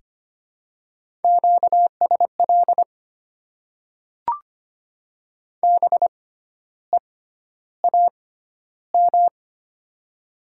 Morse code, heard as a keyed tone, about 25 wpm, spaced as for 12 wpm, 700 Hz. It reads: QSL BEAM
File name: Top 90 QSO Elements - 6x Character Spacing - 25wpm